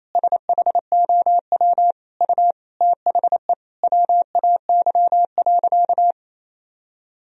Morse code, heard as a keyed tone, about 28 wpm, 700 Hz. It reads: SHOW U THE WAY.
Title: Sentences from Top 300 Words and Abbreviations 28wpm